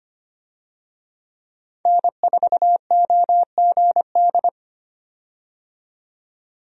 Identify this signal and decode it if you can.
Morse code, heard as a keyed tone, about 25 wpm, 700 Hz. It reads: N4OGD
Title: USA Call Signs - 5 Characters - 25wpm